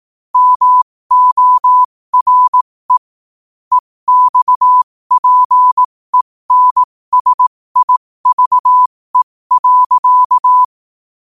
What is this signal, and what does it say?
Morse code, heard as a keyed tone, about 18 wpm, spaced as for 15 wpm, 1000 Hz. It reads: MORE EXPENSIVE.
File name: qotdfiles / 15wpm